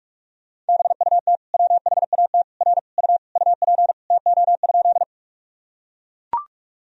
Morse code, heard as a keyed tone, about 41 wpm, 700 Hz. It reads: BUT WHAT R U UP TO?
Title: Sentences from Top 100 Words and Abbreviations 45wpm